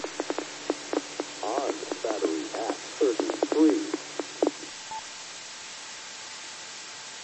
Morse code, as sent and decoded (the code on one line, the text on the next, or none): none